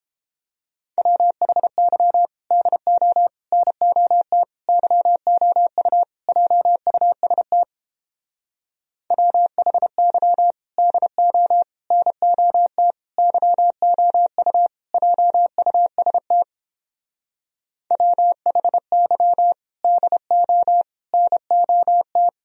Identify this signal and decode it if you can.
Morse code, heard as a keyed tone, about 31 wpm, 700 Hz. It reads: WHY DO NOT YOU JUST WHY DO NOT YOU JUST WHY DO NOT